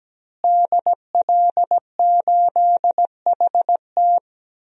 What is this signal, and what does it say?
Morse code, heard as a keyed tone, about 17 wpm, 700 Hz. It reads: DL8HT
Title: DXCC-100 Call Signs - 5 Characters - Repeated 3x - 17wpm